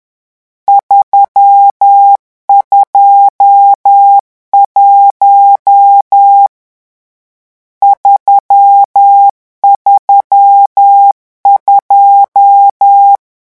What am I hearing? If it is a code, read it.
Morse code, heard as a keyed tone, about 10 wpm, 800 Hz. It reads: 321 332
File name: morse mp3 / lecon10-vitesse 10